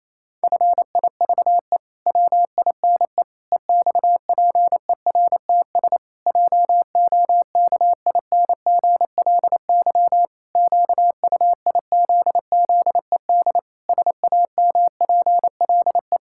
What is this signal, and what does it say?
Morse code, heard as a keyed tone, about 27 wpm, 700 Hz. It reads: FIVE WINE EXPERTS JOKINGLY QUIZZED SAMPLE
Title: Pangram Challenge v2 - 28wpm